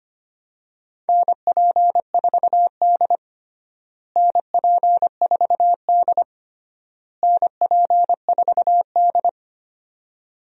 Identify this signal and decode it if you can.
Morse code, heard as a keyed tone, about 25 wpm, 700 Hz. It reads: NP4D NP4D NP4D
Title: USA Call Signs - 4 Characters - Repeated 3x - 25wpm